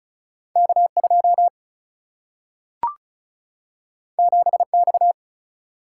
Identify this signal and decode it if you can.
Morse code, heard as a keyed tone, about 36 wpm, 700 Hz. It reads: K2 7X
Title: Single-Letter Single-Number 35wpm